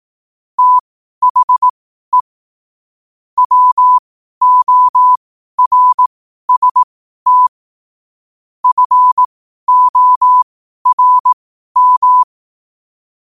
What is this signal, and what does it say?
Morse code, heard as a keyed tone, about 18 wpm, spaced as for 12 wpm, 1000 Hz. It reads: THE WORST FORM
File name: qotdfiles / 12wpm